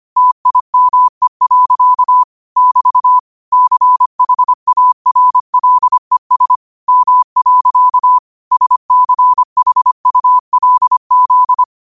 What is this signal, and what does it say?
Morse code, heard as a keyed tone, about 25 wpm, 1000 Hz. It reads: TIME. <BT> CHARLES M. SCHULZ
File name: qotdfiles / 25wpm